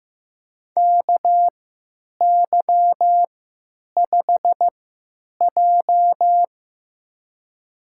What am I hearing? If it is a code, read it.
Morse code, heard as a keyed tone, about 15 wpm, spaced as for 9 wpm, 700 Hz. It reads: KY5J
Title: USA Call Signs - 3x Character Spacing - 15wpm